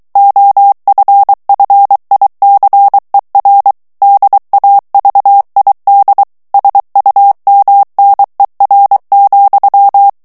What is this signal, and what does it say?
Morse code, heard as a keyed tone, about 23 wpm, 800 Hz. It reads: OFFICER DAVID SUMNER,
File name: ANGLAIS texte clair 3 vitesse 22 mots minute